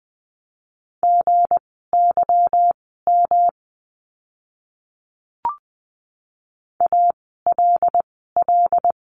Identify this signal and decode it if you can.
Morse code, heard as a keyed tone, about 19 wpm, spaced as for 14 wpm, 700 Hz. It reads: GYM ALL
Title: 160 Lesson 26f - Letter g - Words 20wpm